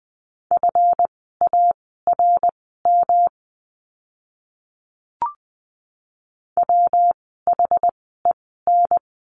Morse code, heard as a keyed tone, about 20 wpm, spaced as for 14 wpm, 700 Hz. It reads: FARM WHEN
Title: 139 Lesson 23f - Letter f - Words 20wpm